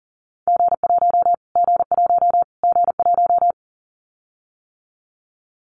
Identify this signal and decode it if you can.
Morse code, heard as a keyed tone, about 40 wpm, 700 Hz. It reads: G1 G1 G1
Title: Callsign Prefixes for Top 25 DXCC - Repeated 3x - 40wpm